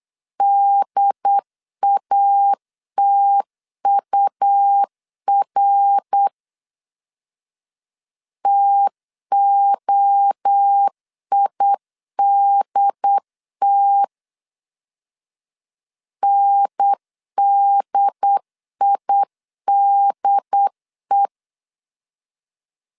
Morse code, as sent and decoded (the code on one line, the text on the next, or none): -.. .- - ..- .-. / - --- .. -.. - / -. -.. .. -.. .
DATUR TOIDT NDIDE